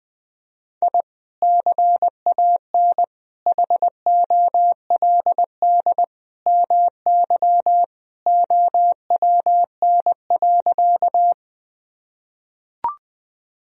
Morse code, heard as a keyed tone, about 21 wpm, 700 Hz. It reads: I CAN HOLD MY OWN.
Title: Sentences from Top 1000 Words - Review Part 2 - 20wpm